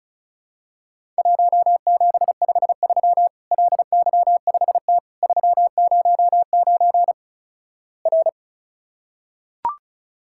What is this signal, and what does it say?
Morse code, heard as a keyed tone, about 34 wpm, 700 Hz. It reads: 1753 LY5T 309 R
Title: Worked All Europe DX Contest - QTC Exchange - 35wpm